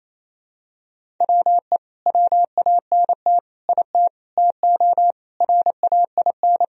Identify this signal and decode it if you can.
Morse code, heard as a keyed tone, about 28 wpm, 700 Hz. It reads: WE WANT IT TO RAIN
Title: Sentences from Top 1000 Words - Review Part 1 - 28wpm